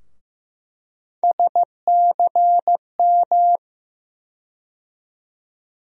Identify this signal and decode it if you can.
Morse code, heard as a keyed tone, about 15 wpm, 700 Hz. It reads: SCM
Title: USA Call Sign Postfixes - 3 Letters 15wpm